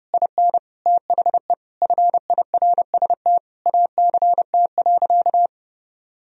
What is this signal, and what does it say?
Morse code, heard as a keyed tone, about 30 wpm, 700 Hz. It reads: IN THE FIRST ACT.
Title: Sentences from Top 1000 Words - Review Part 2 - 30wpm